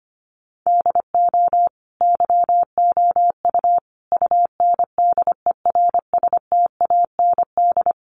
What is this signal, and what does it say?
Morse code, heard as a keyed tone, about 25 wpm, 700 Hz. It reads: DO YOU UNDERSTAND
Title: Sentences from Top 600 Words 25wpm